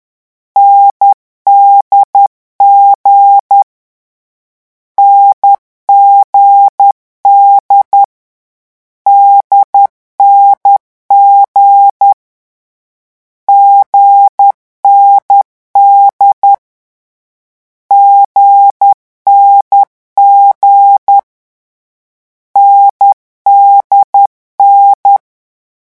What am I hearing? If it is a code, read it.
Morse code, heard as a keyed tone, about 10 wpm, 800 Hz. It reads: NDG NGD DNG GND GNG NDN
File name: morse mp3 / lecon4-vitesse 10